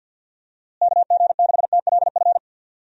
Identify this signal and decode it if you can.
Morse code, heard as a keyed tone, about 49 wpm, 700 Hz. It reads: KG6TLF
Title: USA Call Signs - 6 Characters - Repeated 3x - 50wpm